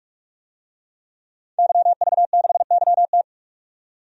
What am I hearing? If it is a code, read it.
Morse code, heard as a keyed tone, about 41 wpm, 700 Hz. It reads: YV6YT
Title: DXCC-100 Call Signs - 5 Characters - 45wpm